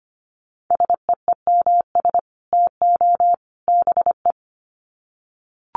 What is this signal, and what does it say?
Morse code, heard as a keyed tone, about 25 wpm, 700 Hz. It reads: SEEMS TO BE
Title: Sets of 3 Words 25wpm